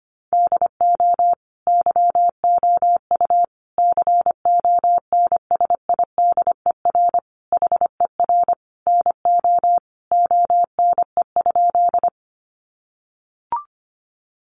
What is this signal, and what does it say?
Morse code, heard as a keyed tone, about 25 wpm, 700 Hz. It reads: DO YOU CONSIDER HER NO ONE?
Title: Sentences from Top 600 Words 25wpm